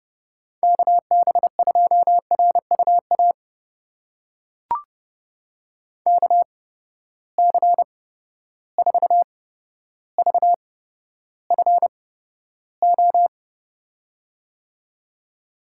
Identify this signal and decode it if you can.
Morse code, heard as a keyed tone, about 30 wpm, spaced as for 13 wpm, 700 Hz. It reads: KB2RUA KC4VFO
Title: USA Call Signs - 8x Character Spacing - 30wpm